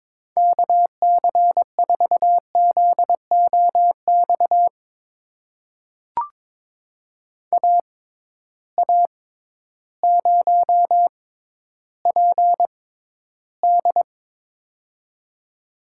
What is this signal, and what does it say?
Morse code, heard as a keyed tone, about 22 wpm, spaced as for 12 wpm, 700 Hz. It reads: KC4ZOX AA0PD
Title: USA Call Signs - 6x Character Spacing - 22wpm